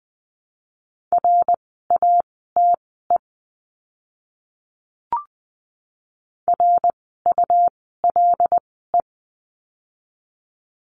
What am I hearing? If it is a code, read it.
Morse code, heard as a keyed tone, about 20 wpm, spaced as for 14 wpm, 700 Hz. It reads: RATE RULE
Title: 101 Lesson 17f - Letter u - Words 20wpm